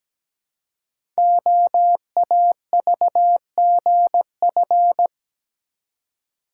Morse code, heard as a keyed tone, about 17 wpm, 700 Hz. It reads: OAVGF
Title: Code Groups - 5 Letters - 17wpm